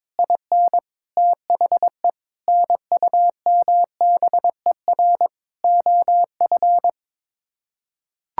Morse code, heard as a keyed tone, about 22 wpm, 700 Hz. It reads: IN THE NUMBER OF
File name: practice / 22wpm / Sets of 6 Words - 22wpm